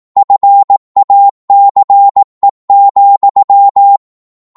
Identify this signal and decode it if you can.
Morse code, heard as a keyed tone, about 18 wpm, 800 Hz. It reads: FACE,
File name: 18wpm